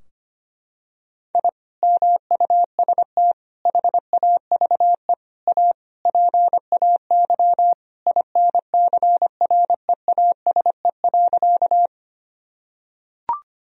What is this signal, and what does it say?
Morse code, heard as a keyed tone, about 25 wpm, 700 Hz. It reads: I MUST HAVE A PAY INCREASE.
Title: Sentences from Top 1000 Words - Review Part 4 - 25wpm